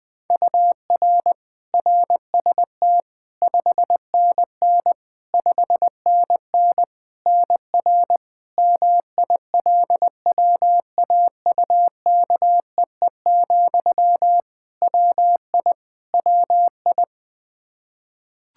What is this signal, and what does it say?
Morse code, heard as a keyed tone, about 20 wpm, 700 Hz. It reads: UR RST 5NN 5NN NR MILWAUKEE, WI WI
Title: Practice QSO - Exchange 1 - Beginning - 20wpm